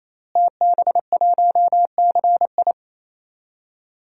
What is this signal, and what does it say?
Morse code, heard as a keyed tone, about 27 wpm, 700 Hz. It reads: TB1CI
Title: DXCC-100 Call Signs - 5 Characters - Repeated 3x - 28wpm